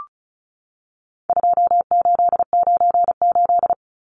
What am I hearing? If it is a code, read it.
Morse code, heard as a keyed tone, about 34 wpm, 700 Hz. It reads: 2898
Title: Numbers - 4-Digit 35wpm